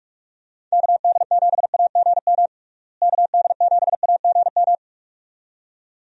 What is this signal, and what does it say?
Morse code, heard as a keyed tone, about 48 wpm, 700 Hz. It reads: KD7AGM KD7AGM
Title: USA Call Signs - 6 Characters - Repeated 2x - 45wpm